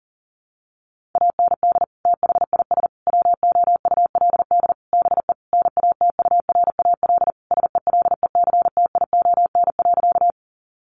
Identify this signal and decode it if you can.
Morse code, heard as a keyed tone, about 40 wpm, 700 Hz. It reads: AND THIS WOULD BE NATURAL SELECTION.
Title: Sentences from Top 1400 Words 40wpm